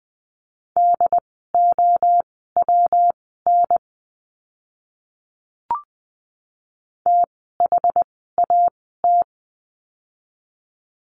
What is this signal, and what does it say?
Morse code, heard as a keyed tone, about 20 wpm, spaced as for 15 wpm, 700 Hz. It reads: DOWN THAT
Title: 160 Lesson 26f - Letter g - Words 20wpm